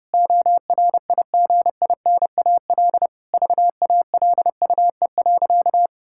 Morse code, heard as a keyed tone, about 30 wpm, 700 Hz. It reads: ORIGINAL VALUE.
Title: Sentences from Top 1000 Words - Review Part 6 - 30wpm